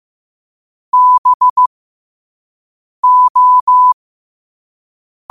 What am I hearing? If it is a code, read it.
Morse code, heard as a keyed tone, about 15 wpm, spaced as for 5 wpm, 1000 Hz. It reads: BO
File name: qotdfiles / 5wpm